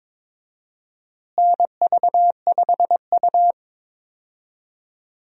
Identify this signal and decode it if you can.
Morse code, heard as a keyed tone, about 22 wpm, 700 Hz. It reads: NV5U
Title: USA Call Signs - 4 Characters - Repeated 3x - 22wpm